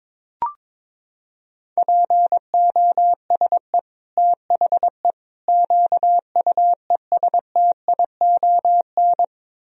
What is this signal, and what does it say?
Morse code, heard as a keyed tone, about 21 wpm, 700 Hz. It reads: POSE THE QUESTION
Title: Sentences from Top 1000 Words - Review Part 2 - 22wpm